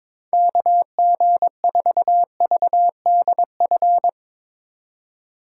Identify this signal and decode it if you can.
Morse code, heard as a keyed tone, about 24 wpm, 700 Hz. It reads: KG4VDF
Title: Speed Racing - USA 2x3 Call Signs 22wpm